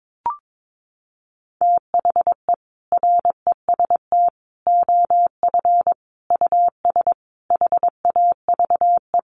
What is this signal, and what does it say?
Morse code, heard as a keyed tone, about 22 wpm, 700 Hz. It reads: THE REST OF US HAVE
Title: Sentences from Top 700 Words 22wpm